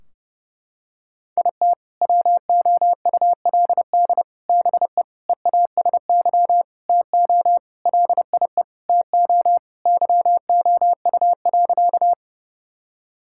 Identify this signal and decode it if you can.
Morse code, heard as a keyed tone, about 30 wpm, 700 Hz. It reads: IT WOULD BE EASY TO LIE TO YOU.
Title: Sentences from Top 1600 Words 30wpm